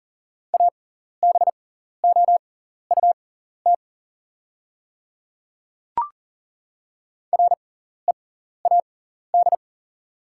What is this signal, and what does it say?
Morse code, heard as a keyed tone, about 39 wpm, spaced as for 14 wpm, 690 Hz. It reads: ABOUT READ
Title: 6x Character Spacing - Top 200 Words - 40wpm